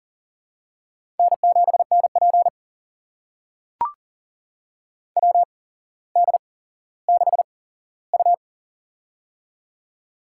Morse code, heard as a keyed tone, about 40 wpm, spaced as for 17 wpm, 700 Hz. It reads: N7NP WD6U
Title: USA Call Signs - 4 Characters - 8x Character Spacing - 40wpm